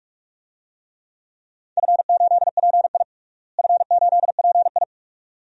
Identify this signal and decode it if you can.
Morse code, heard as a keyed tone, about 41 wpm, 700 Hz. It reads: F8PI F8PI
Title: DXCC-100 Call Signs - 4 Characters - Repeated 2x - 45wpm